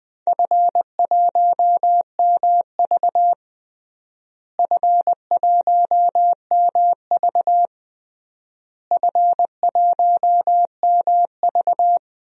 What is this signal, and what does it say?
Morse code, heard as a keyed tone, about 20 wpm, 700 Hz. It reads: F1MV F1MV F1MV
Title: DXCC-100 Call Signs - 4 Characters - Repeated 3x - 20wpm